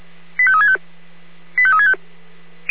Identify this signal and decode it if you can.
Five-tone selective call (ZVEI), heard as quick runs of five tones, tones of about 70 ms. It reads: 85365 85365